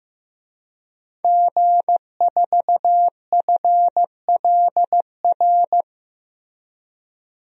Morse code, heard as a keyed tone, about 15 wpm, 700 Hz. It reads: G4FLR
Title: DXCC-100 Call Signs - 5 Characters - Repeated 3x - 15wpm